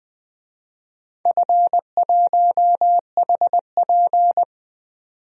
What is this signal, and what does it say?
Morse code, heard as a keyed tone, about 20 wpm, 700 Hz. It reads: F1HP
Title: DXCC-100 Call Signs - 4 Characters - 20wpm